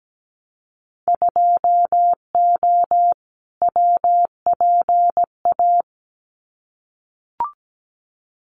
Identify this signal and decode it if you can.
Morse code, heard as a keyed tone, about 17 wpm, 700 Hz. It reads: 2O WPA
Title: Winter Field Day Exchanges 17wpm